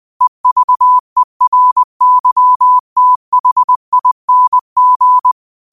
Morse code, heard as a keyed tone, about 20 wpm, 1000 Hz. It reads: EVERYTHING
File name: qotdfiles / 20wpm